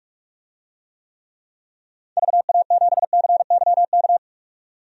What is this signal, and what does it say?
Morse code, heard as a keyed tone, about 43 wpm, 700 Hz. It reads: VA7CYK
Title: DXCC-100 Call Signs - 6 Characters - 45wpm